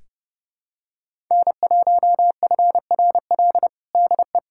Morse code, heard as a keyed tone, about 30 wpm, 690 Hz. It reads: N1FRL DE
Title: USA Callsign DE Callsign 30wpm